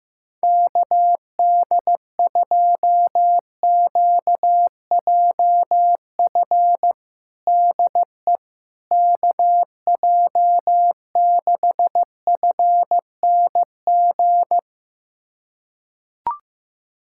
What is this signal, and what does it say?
Morse code, heard as a keyed tone, about 16 wpm, 700 Hz. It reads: KD2QJF DE KJ6FNG
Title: USA Callsign DE Callsign - 6 Characters - 15wpm